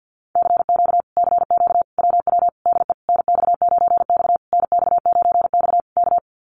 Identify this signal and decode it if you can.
Morse code, heard as a keyed tone, about 48 wpm, 700 Hz. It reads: CQ CQ WW DE NX9X NX9X K
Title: Common QSO Phrases Encore 50wpm